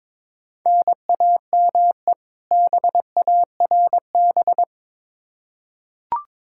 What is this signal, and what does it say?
Morse code, heard as a keyed tone, about 22 wpm, 700 Hz. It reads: NAME BARB
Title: K1USN SST QSO Practice 22wpm